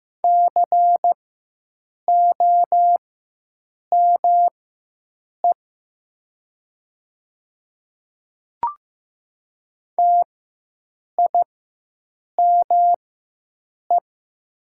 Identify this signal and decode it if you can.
Morse code, heard as a keyed tone, about 15 wpm, spaced as for 7 wpm, 700 Hz. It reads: COME TIME
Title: 4x Character Spacing - Top 100 Words - 15wpm